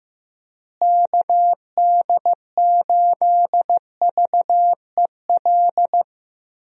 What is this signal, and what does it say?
Morse code, heard as a keyed tone, about 15 wpm, 700 Hz. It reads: KD8VEL
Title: USA Call Signs - 6 Characters - Repeated 3x - 15wpm